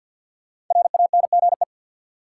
Morse code, heard as a keyed tone, about 52 wpm, 700 Hz. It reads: RANGE
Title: Five Letter Words - 50wpm